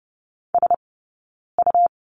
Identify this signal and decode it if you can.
Morse code, heard as a keyed tone, about 30 wpm, 700 Hz. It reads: S U
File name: Lesson 17d - Letter u - Two Characters 30wpm